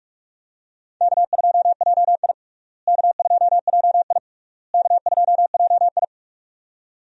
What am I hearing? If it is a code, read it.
Morse code, heard as a keyed tone, about 46 wpm, 700 Hz. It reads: K2JI K2JI K2JI
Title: USA Call Signs - 4 Characters - Repeated 3x - 45wpm